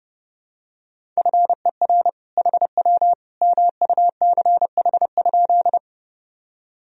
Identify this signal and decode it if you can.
Morse code, heard as a keyed tone, about 32 wpm, 700 Hz. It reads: FER HW MUCH?
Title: Sentences from Top 200 Words and Abbreviations 30wpm